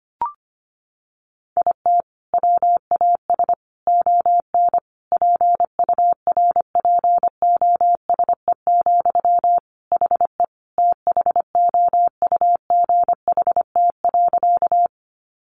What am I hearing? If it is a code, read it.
Morse code, heard as a keyed tone, about 25 wpm, 700 Hz. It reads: IT WAS ON PURPOSE, HE THOUGHT.
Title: Sentences from Top 800 Words 25wpm